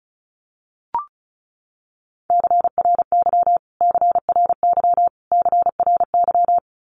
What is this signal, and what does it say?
Morse code, heard as a keyed tone, about 34 wpm, 700 Hz. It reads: CRY CRY CRY
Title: Top 500 Words Repeated 35wpm